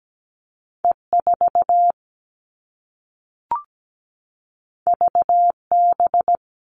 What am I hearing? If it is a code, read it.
Morse code, heard as a keyed tone, about 17 wpm, 700 Hz. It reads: E4 VB
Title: All Cut Numbers - 2-Digit 17wpm